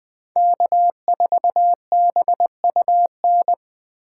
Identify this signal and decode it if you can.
Morse code, heard as a keyed tone, about 20 wpm, 700 Hz. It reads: K4BUN
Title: USA Call Signs - 8x Character Spacing - 20wpm